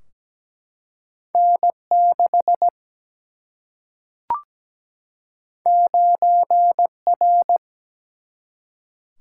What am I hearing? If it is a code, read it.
Morse code, heard as a keyed tone, about 17 wpm, 700 Hz. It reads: N 6 9 R
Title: Single-Letter Single-Number 17wpm